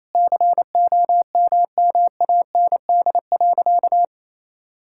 Morse code, heard as a keyed tone, about 28 wpm, 700 Hz. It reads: COMMAND.
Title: Sentences from Top 1600 Words 28wpm